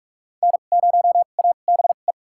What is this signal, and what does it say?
Morse code, heard as a keyed tone, about 43 wpm, spaced as for 32 wpm, 700 Hz. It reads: N0ABE
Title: USA Call Signs - 5 Characters - 2x Character Spacing - 45wpm